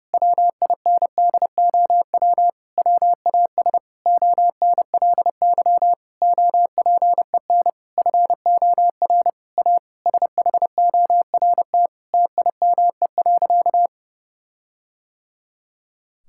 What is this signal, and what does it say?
Morse code, heard as a keyed tone, about 30 wpm, 700 Hz. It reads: WINDOW WAS ONLY OPEN FOR A SHORT TIME.
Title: Sentences from Top 1600 Words 30wpm